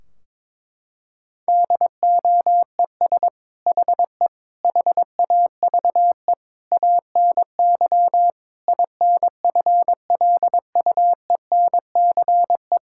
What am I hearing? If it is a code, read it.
Morse code, heard as a keyed tone, about 22 wpm, 700 Hz. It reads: DOES HE HAVE ANY INFLUENCE
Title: Sentences from Top 1600 Words 22wpm